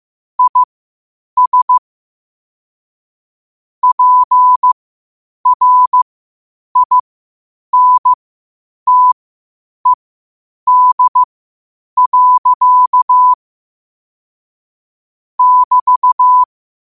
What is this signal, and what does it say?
Morse code, heard as a keyed tone, about 15 wpm, spaced as for 8 wpm, 1000 Hz. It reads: IS PRINTED. <BT>
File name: qotdfiles / 8wpm